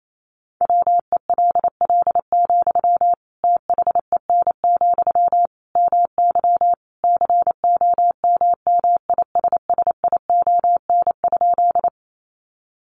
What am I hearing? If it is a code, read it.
Morse code, heard as a keyed tone, about 28 wpm, 700 Hz. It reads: WELL, THEN, MY COMMISSION?
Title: Sentences from Top 1400 Words 28wpm